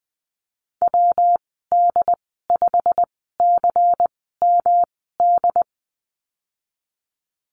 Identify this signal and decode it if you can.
Morse code, heard as a keyed tone, about 20 wpm, spaced as for 15 wpm, 700 Hz. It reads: WD5CMD
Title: 179 Lesson 29g - Slash - Call Signs 20wpm